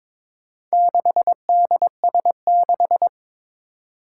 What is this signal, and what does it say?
Morse code, heard as a keyed tone, about 22 wpm, 700 Hz. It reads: 6DS6
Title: Four Challenging Letter-Number Combinations 22wpm